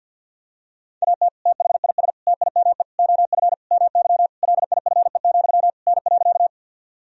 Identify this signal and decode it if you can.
Morse code, heard as a keyed tone, about 53 wpm, 700 Hz. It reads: AT THIS TIME OF MY LIFE, N.
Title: Sentences from Top 300 Words and Abbreviations 50wpm